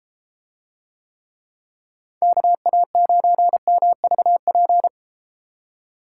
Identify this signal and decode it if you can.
Morse code, heard as a keyed tone, about 32 wpm, 700 Hz. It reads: KA9MVP